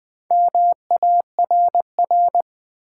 Morse code, heard as a keyed tone, about 20 wpm, 700 Hz. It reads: MARR
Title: Build A Word - 1 Letter At A Time - 5-Letter Words - 20wpm